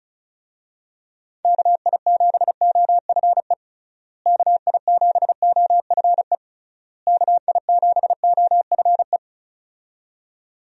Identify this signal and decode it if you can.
Morse code, heard as a keyed tone, about 36 wpm, 700 Hz. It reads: KI7OFE KI7OFE KI7OFE
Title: USA Call Signs - 6 Characters - Repeated 3x - 35wpm